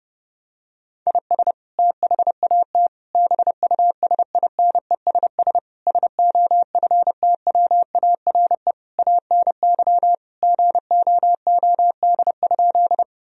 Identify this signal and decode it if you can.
Morse code, heard as a keyed tone, about 32 wpm, 700 Hz. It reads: IS THAT BUSINESS SOFTWARE ANY GOOD?
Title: Sentences from Top 1000 Words - Review Part 3 - 30wpm